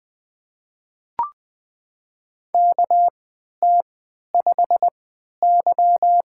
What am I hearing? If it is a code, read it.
Morse code, heard as a keyed tone, about 20 wpm, spaced as for 11 wpm, 700 Hz. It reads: KT5Y
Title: USA Call Signs - 4 Characters - 3x Character Spacing - 20wpm